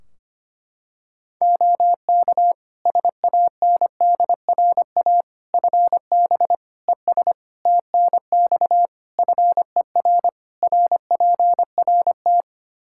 Morse code, heard as a keyed tone, about 25 wpm, 700 Hz. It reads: OK SANDRA FB ES TNX FER RPRT
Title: Practice QSO - Exchange 2 - Info - 25wpm